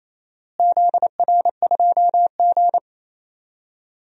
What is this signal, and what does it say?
Morse code, heard as a keyed tone, about 28 wpm, 700 Hz. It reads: ZR2G
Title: DXCC-100 Call Signs - 4 Characters - Repeated 3x - 28wpm